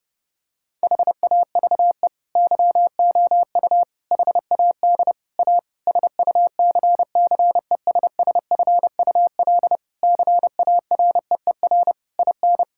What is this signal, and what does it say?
Morse code, heard as a keyed tone, about 30 wpm, 700 Hz. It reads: HAVE YOU HAD A SUCCESSFUL CAREER IN
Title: Sentences from Top 1000 Words - Review Part 6 - 30wpm